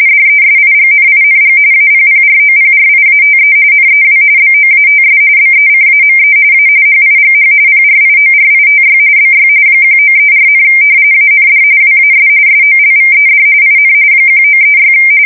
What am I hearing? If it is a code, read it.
RTTY, about 45.45 baud. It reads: DESIGNING APPROPRIATE NEW BOATS BECAME A HIGH PRIORITY BY 1934, AND WAS SOLVED IN 1936
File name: fsk170 / usnavy4